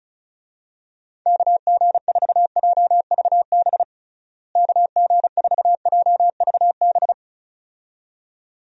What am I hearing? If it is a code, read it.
Morse code, heard as a keyed tone, about 34 wpm, 700 Hz. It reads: KG4JVB KG4JVB
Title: USA Call Signs - 6 Characters - Repeated 2x - 35wpm